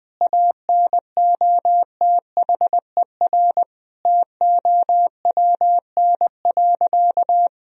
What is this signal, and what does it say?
Morse code, heard as a keyed tone, about 20 wpm, 700 Hz. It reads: ANOTHER TOWN.
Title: Sentences from Top 1000 Words - Review Part 2 - 20wpm